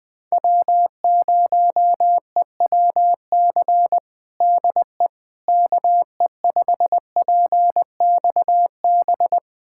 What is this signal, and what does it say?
Morse code, heard as a keyed tone, about 20 wpm, 690 Hz. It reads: W0EWC DE KE5PXB
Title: USA Callsign DE Callsign 20wpm